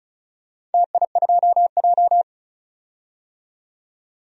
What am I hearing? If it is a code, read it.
Morse code, heard as a keyed tone, about 36 wpm, 700 Hz. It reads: TI2J
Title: DXCC-100 Call Signs - 4 Characters - 35wpm